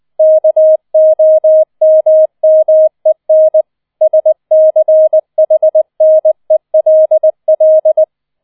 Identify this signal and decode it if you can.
Morse code, heard as a keyed tone, about 20 wpm, 630 Hz. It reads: KOMMEN SCHNELL